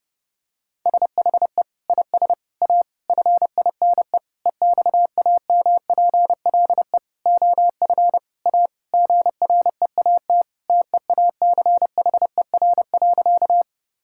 Morse code, heard as a keyed tone, about 30 wpm, 700 Hz. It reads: SHE IS A FINE EXAMPLE OF A GREAT TEACHER.
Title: Sentences from Top 1000 Words - Review Part 6 - 30wpm